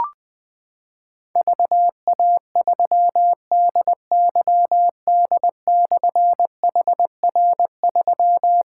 Morse code, heard as a keyed tone, about 20 wpm, 700 Hz. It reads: VA3DYD/HR3
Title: Difficult Call Signs 20wpm